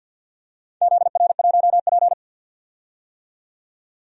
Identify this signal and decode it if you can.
Morse code, heard as a keyed tone, about 51 wpm, 700 Hz. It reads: ZR1P
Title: DXCC-100 Call Signs - 4 Characters - 50wpm